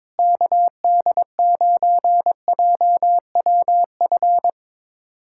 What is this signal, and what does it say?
Morse code, heard as a keyed tone, about 21 wpm, 700 Hz. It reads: KD9JWF
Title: USA Callsign DE Callsign - 6 Characters - 22wpm